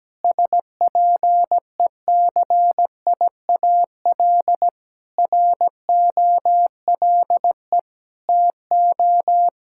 Morse code, heard as a keyed tone, about 17 wpm, 700 Hz. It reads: SPECIAL ROLE TO
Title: Sentences from Top 1000 Words - Review Part 6 - 17wpm